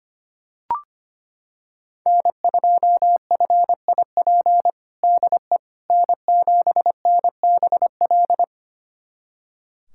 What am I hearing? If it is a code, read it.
Morse code, heard as a keyed tone, about 25 wpm, 700 Hz. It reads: N2FIP DE N7NBL
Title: USA Callsign DE Callsign - 5 Characters - 25wpm